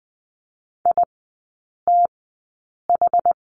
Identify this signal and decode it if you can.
Morse code, heard as a keyed tone, about 20 wpm, spaced as for 8 wpm, 690 Hz. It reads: ITH
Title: 093 Lesson 16e - Letter c - Three Characters 20wpm